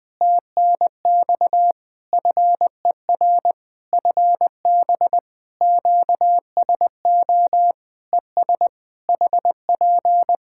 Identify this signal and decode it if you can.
Morse code, heard as a keyed tone, about 20 wpm, 700 Hz. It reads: TNX FER FB QSO ES HP
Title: Practice QSO - Exchange 4 - Ending - 20wpm